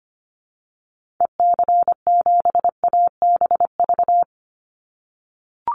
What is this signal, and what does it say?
Morse code, heard as a keyed tone, about 25 wpm, 700 Hz. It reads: EC7ABV
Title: Call Signs - 25wpm